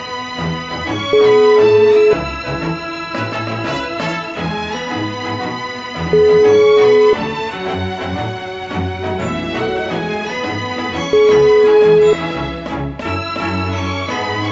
Ringback tone (Europe, 425 Hz), with noise.